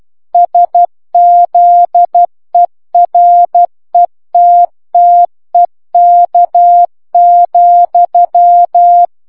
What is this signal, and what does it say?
Morse code, heard as a keyed tone, about 12 wpm, 700 Hz. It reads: SZERETTEK,